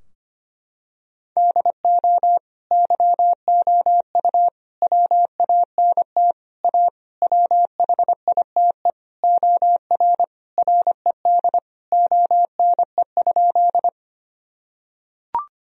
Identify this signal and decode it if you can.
Morse code, heard as a keyed tone, about 25 wpm, 700 Hz. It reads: DO YOU WANT A WHITE OR RED ONE?
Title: Sentences from Top 1000 Words - Review Part 2 - 25wpm